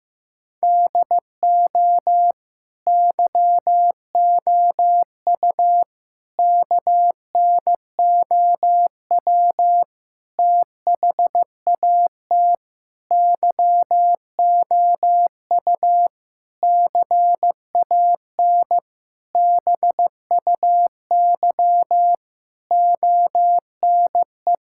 Morse code, heard as a keyed tone, about 15 wpm, 700 Hz. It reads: DO YOU KNOW THAT YOU CAN BUY ONE
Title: Sentences from Top 1000 Words - Review Part 5 - 15wpm